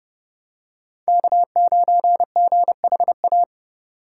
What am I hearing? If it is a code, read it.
Morse code, heard as a keyed tone, about 32 wpm, 700 Hz. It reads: K9GHA
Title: USA Call Signs - 5 Characters - 3x Character Spacing - 30wpm